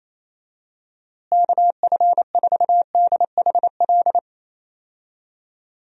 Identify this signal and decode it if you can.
Morse code, heard as a keyed tone, about 28 wpm, 700 Hz. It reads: KF4DHL
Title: USA Call Signs - 6 Characters - Repeated 2x - 28wpm